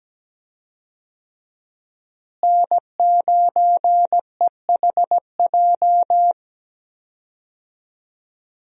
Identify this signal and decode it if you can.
Morse code, heard as a keyed tone, about 17 wpm, 700 Hz. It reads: N9EHJ
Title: USA Call Signs - 5 Characters - Repeated 2x - 17wpm